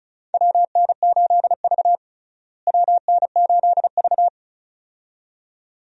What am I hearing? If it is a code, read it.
Morse code, heard as a keyed tone, about 34 wpm, 700 Hz. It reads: WN8V WN8V
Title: USA Call Signs - 4 Characters - Repeated 2x - 35wpm